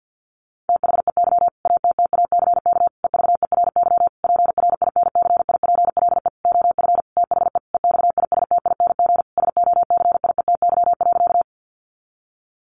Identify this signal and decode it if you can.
Morse code, heard as a keyed tone, about 50 wpm, 700 Hz. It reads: THEY ATTACK EVERY PRINCIPLE OF THE EXISTING SOCIETY.
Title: Sentences from Top 1500 Words 50wpm